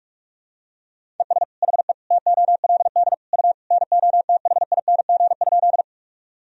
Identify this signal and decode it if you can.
Morse code, heard as a keyed tone, about 46 wpm, 700 Hz. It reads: ES HE TOLD U NOTHING?
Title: Sentences from Top 500 Words and Abbreviations 45wpm